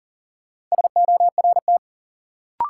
Morse code, heard as a keyed tone, about 40 wpm, 700 Hz. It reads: SORT
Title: Sentences from Top 1000 Words - Review Part 3 - New Word Drill - 40wpm